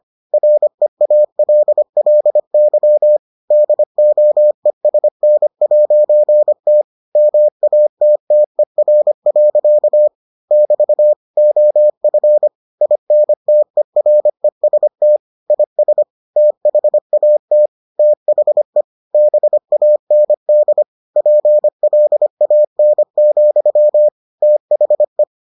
Morse code, heard as a keyed tone, about 25 wpm, 600 Hz. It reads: REALLY DOESN'T MATTER. <BT> OF INTEREST IS THAT THE BAND PLAN, THE